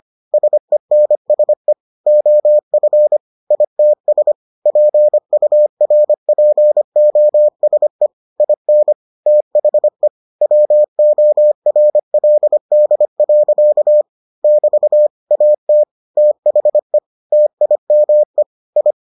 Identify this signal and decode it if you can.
Morse code, heard as a keyed tone, about 25 wpm, 600 Hz. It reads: SENSE OF ITS PURPOSE IN THE WORLD. <BT> AT THE TIME I